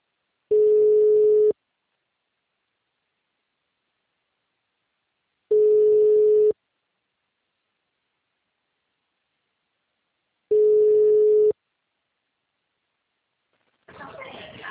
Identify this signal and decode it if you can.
Ringback tone (Europe, 425 Hz), with speech.